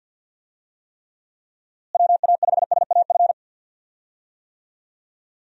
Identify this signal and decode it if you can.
Morse code, heard as a keyed tone, about 48 wpm, 700 Hz. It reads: WA5SAF
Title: USA Call Signs - 6 Characters - Repeated 2x - 50wpm